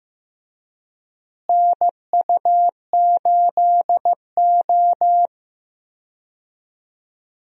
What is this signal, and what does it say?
Morse code, heard as a keyed tone, about 16 wpm, 700 Hz. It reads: NU8O
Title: USA Call Signs - 4 Characters - Repeated 3x - 15wpm